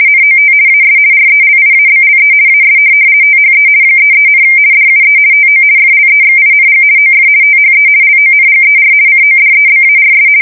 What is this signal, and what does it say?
RTTY, about 50 baud. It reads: NOT AS RELIABLE AS THE 33, DUE TO FLEXING OF THE LONGER CODEBARS
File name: fsk170-50B / teletype-corp